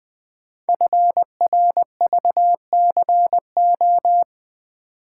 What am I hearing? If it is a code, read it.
Morse code, heard as a keyed tone, about 20 wpm, 700 Hz. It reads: FRVCO
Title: Code Groups - 5 Letters - 20wpm